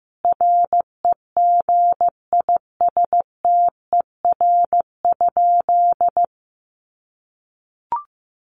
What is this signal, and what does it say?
Morse code, heard as a keyed tone, about 15 wpm, 700 Hz. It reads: REGISTER?
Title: Sentences from Top 1200 Words 15wpm